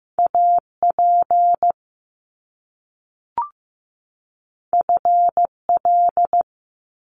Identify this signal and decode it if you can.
Morse code, heard as a keyed tone, about 15 wpm, 700 Hz. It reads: AP FL
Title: Top Two Letter Combinations in Words 15wpm